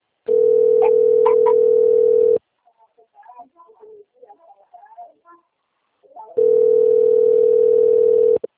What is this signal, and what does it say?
Ringback tone (North America), 440 + 480 Hz.
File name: rg-1600-unknown- - - .8375